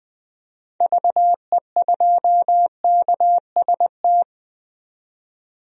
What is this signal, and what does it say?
Morse code, heard as a keyed tone, about 21 wpm, 700 Hz. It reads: VE2KST
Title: DXCC-100 Call Signs - 6 Characters - 20wpm